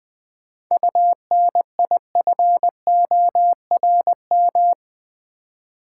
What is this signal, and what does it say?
Morse code, heard as a keyed tone, about 20 wpm, 690 Hz. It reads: UNIFORM
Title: 7 Letter Words - First 3 Letters Given - 20wpm